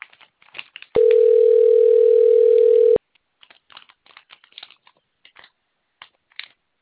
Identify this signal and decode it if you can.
Ringback tone (North America), 440 + 480 Hz.